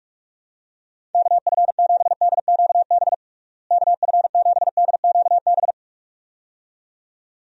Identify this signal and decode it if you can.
Morse code, heard as a keyed tone, about 46 wpm, 700 Hz. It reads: KF7DQB KF7DQB
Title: USA Call Signs - 6 Characters - Repeated 2x - 45wpm